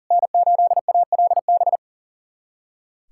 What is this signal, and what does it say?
Morse code, heard as a keyed tone, about 40 wpm, 700 Hz. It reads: N8ALB
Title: USA Callsign DE Callsign - 5 Characters - 40wpm